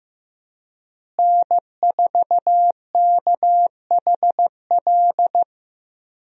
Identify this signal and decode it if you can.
Morse code, heard as a keyed tone, about 15 wpm, 700 Hz. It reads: N4KHL
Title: USA Call Signs - 5 Characters - Repeated 2x - 15wpm